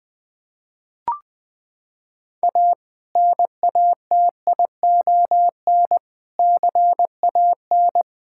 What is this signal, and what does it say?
Morse code, heard as a keyed tone, about 21 wpm, 700 Hz. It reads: A NATION CAN
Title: Sentences from Top 1000 Words - Review Part 6 - 20wpm